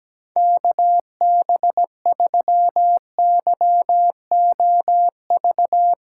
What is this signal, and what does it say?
Morse code, heard as a keyed tone, about 17 wpm, 700 Hz. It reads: KB3YOV
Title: USA Callsign DE Callsign - 6 Characters - 17wpm